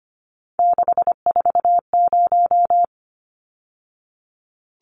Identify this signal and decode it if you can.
Morse code, heard as a keyed tone, about 25 wpm, 700 Hz. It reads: 640
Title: Numbers - 3-Digit 25wpm